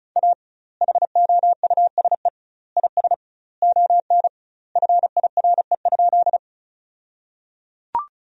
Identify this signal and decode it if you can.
Morse code, heard as a keyed tone, about 36 wpm, 700 Hz. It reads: A HOUSE IS ON FIRE?
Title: 2x Word-Spacing - Sentences from Top 400 Words - 35wpm